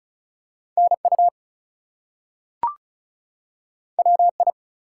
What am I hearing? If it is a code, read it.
Morse code, heard as a keyed tone, about 35 wpm, 700 Hz. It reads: NU WI
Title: US States and Canadian Provinces 35wpm